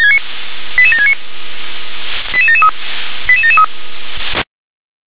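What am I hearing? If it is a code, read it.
Five-tone selective call (ZVEI), heard as quick runs of five tones, tones of about 70 ms. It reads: 88769 88773 88773
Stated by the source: ZVEI five-tone selective call received on SDR.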